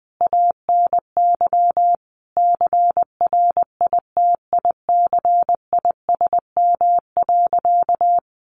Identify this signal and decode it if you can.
Morse code, heard as a keyed tone, about 20 wpm, 700 Hz. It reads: ANY CRITICISM.
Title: Sentences from Top 1300 Words 20wpm